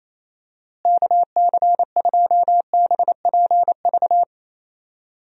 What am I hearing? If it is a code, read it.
Morse code, heard as a keyed tone, about 28 wpm, 700 Hz. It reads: KC2BPV
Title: USA Call Signs - 6 Characters - Repeated 2x - 28wpm